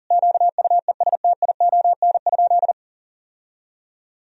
Morse code, heard as a keyed tone, about 37 wpm, 700 Hz. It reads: QUESTION?
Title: Sentences from Top 1000 Words - Review Part 4 - 40wpm